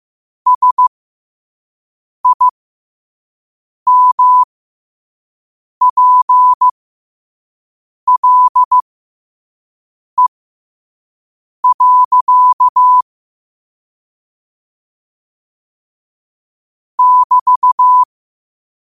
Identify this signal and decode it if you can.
Morse code, heard as a keyed tone, about 15 wpm, spaced as for 5 wpm, 1000 Hz. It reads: SIMPLE. <BT>
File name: qotdfiles / 5wpm